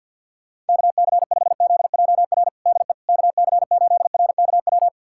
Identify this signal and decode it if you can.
Morse code, heard as a keyed tone, about 51 wpm, 690 Hz. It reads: KC5ZJR DE KC8RKW
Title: USA Callsign DE Callsign - 6 Characters - 50wpm